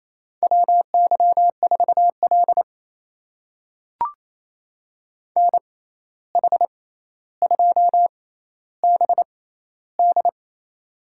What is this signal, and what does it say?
Morse code, heard as a keyed tone, about 28 wpm, spaced as for 13 wpm, 700 Hz. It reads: WY4L NH2BD
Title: USA Call Signs - 6x Character Spacing - 28wpm